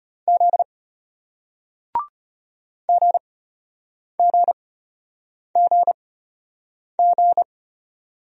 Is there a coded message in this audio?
Yes: Morse code from 0.3 to 7.4 s, heard as a keyed tone, about 33 wpm, spaced as for 8 wpm, 700 Hz. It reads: Z GGGG